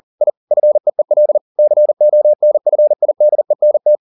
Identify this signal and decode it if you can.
Morse code, heard as a keyed tone, about 40 wpm, 600 Hz. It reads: I FEEL CONFIDENT